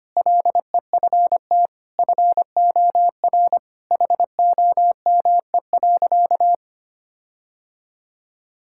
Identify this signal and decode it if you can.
Morse code, heard as a keyed tone, about 25 wpm, 700 Hz. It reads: LEFT FOR HOME.
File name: practice / 25wpm / Sentences with 4-Word Phrases - 25wpm